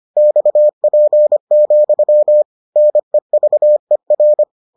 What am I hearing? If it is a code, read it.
Morse code, heard as a keyed tone, about 25 wpm, 600 Hz. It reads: XP, NEVER